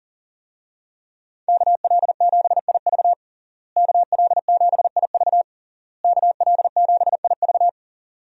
Morse code, heard as a keyed tone, about 40 wpm, 700 Hz. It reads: KL7IV KL7IV KL7IV
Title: USA Call Signs - 5 Characters - Repeated 3x - 40wpm